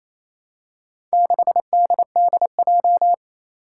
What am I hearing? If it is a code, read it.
Morse code, heard as a keyed tone, about 28 wpm, 700 Hz. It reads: 6DDJ
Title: Four Challenging Letter-Number Combinations 28wpm